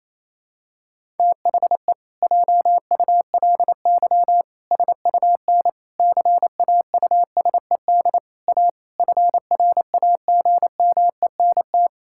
Morse code, heard as a keyed tone, about 28 wpm, 700 Hz. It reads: THE JULY SUN CAUSED A FRAGMENT
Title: Pangram Challenge v2 - 28wpm